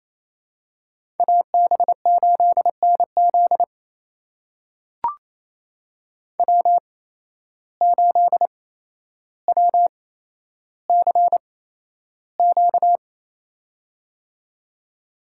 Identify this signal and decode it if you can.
Morse code, heard as a keyed tone, about 27 wpm, spaced as for 12 wpm, 700 Hz. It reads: AB8NZ W8WCQ
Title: USA Call Signs - 5 Characters - 8x Character Spacing - 28wpm